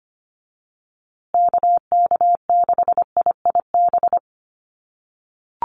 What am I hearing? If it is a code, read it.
Morse code, heard as a keyed tone, about 25 wpm, 700 Hz. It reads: KK6IIB
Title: USA Call Signs Repeated 2x 25wpm